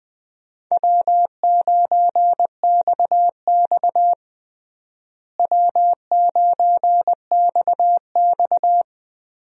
Morse code, heard as a keyed tone, about 20 wpm, 700 Hz. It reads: W9XX W9XX
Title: USA Call Signs - 4 Characters - Repeated 3x - 20wpm